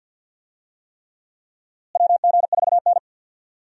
Morse code, heard as a keyed tone, about 48 wpm, 700 Hz. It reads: WG4N
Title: USA Call Signs - 4 Characters - Repeated 3x - 50wpm